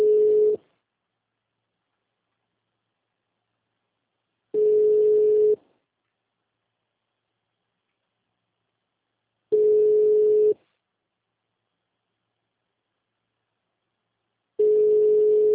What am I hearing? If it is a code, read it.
Ringback tone (Europe, 425 Hz).